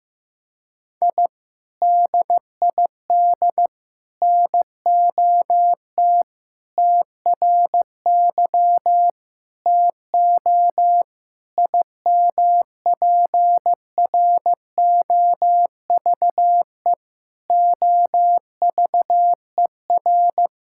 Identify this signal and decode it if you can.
Morse code, heard as a keyed tone, about 15 wpm, 700 Hz. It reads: I DID NOT TRY TO IMPROVE OVER
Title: Sentences from Top 1000 Words - Review Part 5 - 15wpm